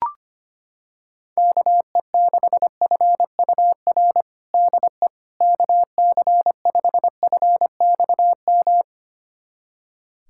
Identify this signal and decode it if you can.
Morse code, heard as a keyed tone, about 25 wpm, 700 Hz. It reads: KE6FUR DE KC5FXM
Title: USA Callsign DE Callsign 25wpm